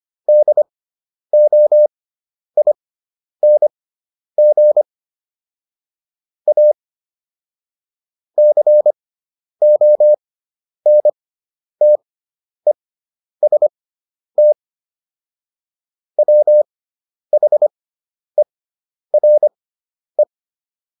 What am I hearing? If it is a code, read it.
Morse code, heard as a keyed tone, about 25 wpm, spaced as for 10 wpm, 600 Hz. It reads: DOING A CONTEST WHERE